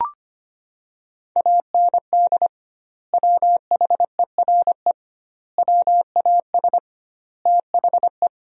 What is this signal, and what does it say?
Morse code, heard as a keyed tone, about 25 wpm, 700 Hz. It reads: AND WHERE WAS THE
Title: 2x Word-Spacing - Sentences from Top 200 Words - 25wpm